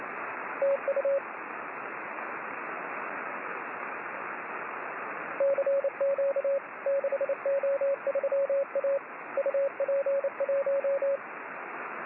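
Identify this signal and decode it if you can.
Morse code, heard as a keyed tone, about 29 wpm, 550 Hz, with radio noise. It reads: TU CQ 6O3A UP1
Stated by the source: Morse signal on the 6 m band.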